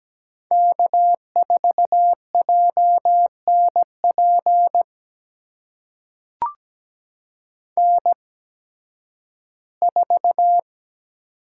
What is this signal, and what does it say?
Morse code, heard as a keyed tone, about 17 wpm, spaced as for 11 wpm, 700 Hz. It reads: K4JNP N4
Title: USA Call Signs - 8x Character Spacing - 17wpm